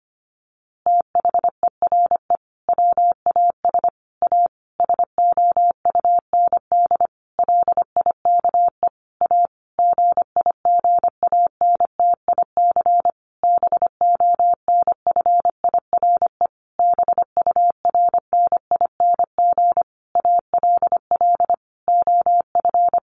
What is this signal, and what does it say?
Morse code, heard as a keyed tone, about 25 wpm, 700 Hz. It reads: THERE WAS A SOUND LIKE A GIGANTIC BONFIRE BURNING ALL OF
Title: A Sound Of Thunder 25wpm